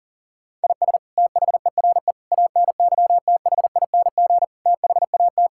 Morse code, heard as a keyed tone, about 40 wpm, 700 Hz. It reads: IS THERE ANYTHING THAT
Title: Sentences from Top 1000 Words - Review Part 6 - 40wpm